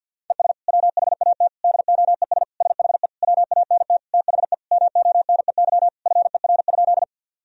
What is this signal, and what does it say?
Morse code, heard as a keyed tone, about 50 wpm, 700 Hz. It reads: ES WHAT DOES SHE WANT THE MONEY FER?
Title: Sentences from Top 400 Words and Abbreviations 50wpm